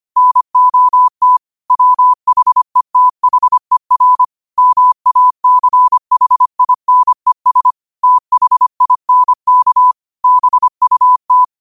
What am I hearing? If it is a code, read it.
Morse code, heard as a keyed tone, about 25 wpm, 1000 Hz. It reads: NOT WHETHER MACHINES THINK BUT